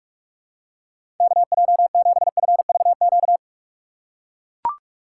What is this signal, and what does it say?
Morse code, heard as a keyed tone, about 44 wpm, 700 Hz. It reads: KJ7FVQ
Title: USA Call Signs - 6 Characters - 2x Character Spacing - 45wpm